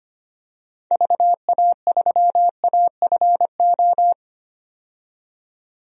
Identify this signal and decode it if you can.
Morse code, heard as a keyed tone, about 25 wpm, 700 Hz. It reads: VA3AFO
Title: DXCC-100 Call Signs - 6 Characters - 25wpm